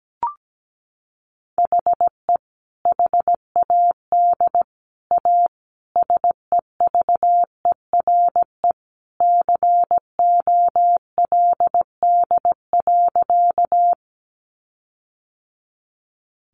Morse code, heard as a keyed tone, about 17 wpm, 700 Hz. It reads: HE HAD A SEVERE COLD.
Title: Sentences from Top 1400 Words 17wpm